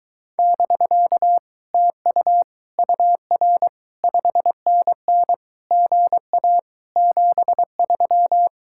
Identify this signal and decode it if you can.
Morse code, heard as a keyed tone, about 22 wpm, 700 Hz. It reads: <BK> TU UR 5NN GA 73